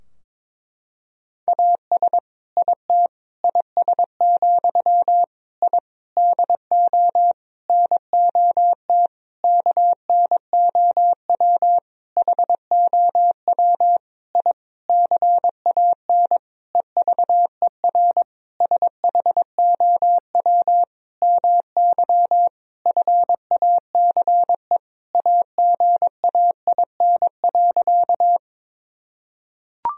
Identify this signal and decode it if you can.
Morse code, heard as a keyed tone, about 22 wpm, 700 Hz. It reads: AS IT IS, I DO NOT KNOW HOW I CAN EVER SHOW MY FACE AGAIN.
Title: Sentences from Top 1000 Words - Review Part 2 - 22wpm